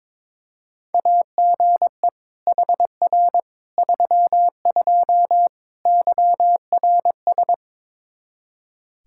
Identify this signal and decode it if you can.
Morse code, heard as a keyed tone, about 22 wpm, 700 Hz. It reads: AGE HR 32 YRS
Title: Practice QSO - Exchange 3 - More Info - 22wpm